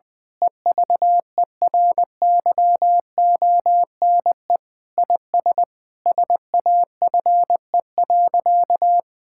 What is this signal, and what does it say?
Morse code, heard as a keyed tone, about 20 wpm, 700 Hz. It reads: EVERYONE IS SAFE.
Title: Sentences with 5-Word Phrases - Spoken Beginning - 20wpm